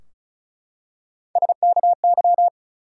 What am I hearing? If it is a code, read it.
Morse code, heard as a keyed tone, about 36 wpm, 700 Hz. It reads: SKY
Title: Three Letter Words - 35wpm